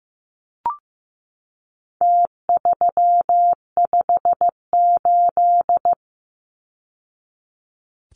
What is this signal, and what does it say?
Morse code, heard as a keyed tone, about 15 wpm, 700 Hz. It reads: T358
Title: Numbers - Contest Style - 4-Digit 15wpm